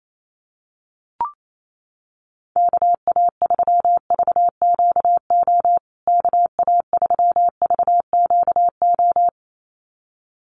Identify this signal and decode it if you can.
Morse code, heard as a keyed tone, about 28 wpm, 700 Hz. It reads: KA3VQO KA3VQO
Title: USA Call Signs Repeated 2x 28wpm